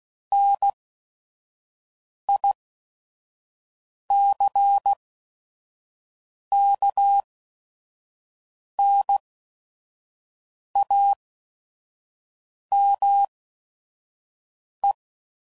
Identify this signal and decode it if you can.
Morse code, heard as a keyed tone, about 16 wpm, spaced as for 5 wpm, 800 Hz. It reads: NICKNAME
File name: CW Code Groups @5WPM / Technical Subjects / AND Gate